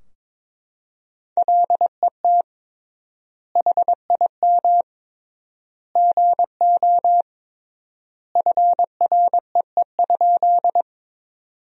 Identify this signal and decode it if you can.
Morse code, heard as a keyed tone, about 22 wpm, 700 Hz. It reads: LET HIM GO FREE?
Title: 3x Word-Spacing - Sentences from Top 500 Words - 22wpm